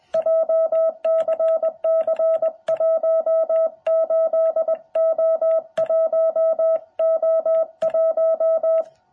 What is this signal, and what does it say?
Morse code, heard as a keyed tone, about 21 wpm, 660 Hz. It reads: JCC18O1O1
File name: jh8cbh / 2021cw